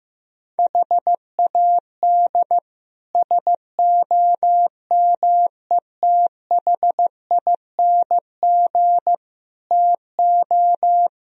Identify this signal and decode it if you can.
Morse code, heard as a keyed tone, about 15 wpm, 700 Hz. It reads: HAD SOMETHING TO
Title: Speed-Racing - Sets of 5 Words 15wpm